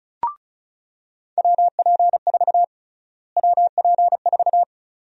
Morse code, heard as a keyed tone, about 36 wpm, 700 Hz. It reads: WP4 WP4
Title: USA Call Sign Prefixes - Repeated 2x - 35wpm